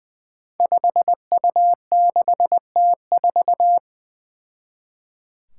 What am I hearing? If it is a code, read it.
Morse code, heard as a keyed tone, about 20 wpm, 700 Hz. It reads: 5U6T4
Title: Five Challenging Letter-Number Combinations 20wpm